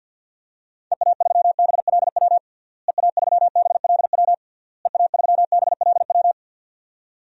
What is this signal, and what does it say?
Morse code, heard as a keyed tone, about 50 wpm, 700 Hz. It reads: EA3BLW EA3BLW EA3BLW
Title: DXCC-100 Call Signs - 6 Characters - Repeated 3x - 50wpm